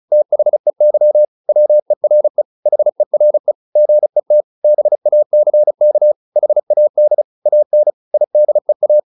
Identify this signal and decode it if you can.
Morse code, heard as a keyed tone, about 34 wpm, 600 Hz. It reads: THEY WERE HERE GET BACK HAD AN IDEA